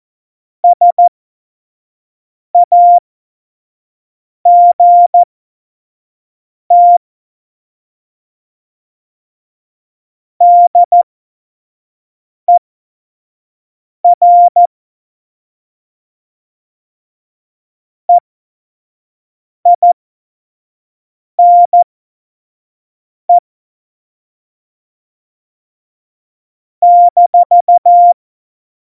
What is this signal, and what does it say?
Morse code, heard as a keyed tone, about 14 wpm, spaced as for 5 wpm, 700 Hz. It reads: SAGT DER EINE -
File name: Morsekurs / Witze / Witz 26. Juni 2018, 25 F70